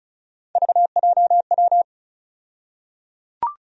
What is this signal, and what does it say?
Morse code, heard as a keyed tone, about 36 wpm, 700 Hz. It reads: VJW
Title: USA Call Sign Postfixes - 3 Letters 35wpm